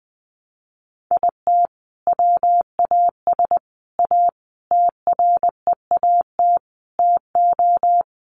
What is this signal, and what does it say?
Morse code, heard as a keyed tone, about 20 wpm, 700 Hz. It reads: IT WAS A TREAT TO
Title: Sentences from Top 1300 Words 20wpm